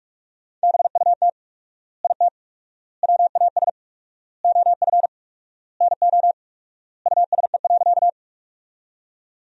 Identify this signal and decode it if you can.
Morse code, heard as a keyed tone, about 46 wpm, 700 Hz. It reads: BUT IT WAS OF NO USE.
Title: 4x Word-Spacing - Sentences from Top 100 Words - 45wpm